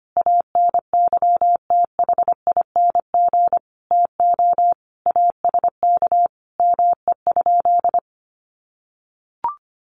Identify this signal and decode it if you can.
Morse code, heard as a keyed tone, about 25 wpm, 700 Hz. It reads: ANYTHING TO ASK ME?
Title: Sentences from Top 1000 Words 25wpm